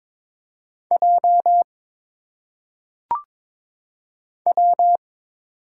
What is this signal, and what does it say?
Morse code, heard as a keyed tone, about 22 wpm, 700 Hz. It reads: J W
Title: Single Letters - Phonetic Pronunciation 22wpm